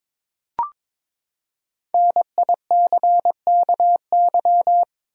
Morse code, heard as a keyed tone, about 22 wpm, 700 Hz. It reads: NICKY
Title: USA Common Names - 22wpm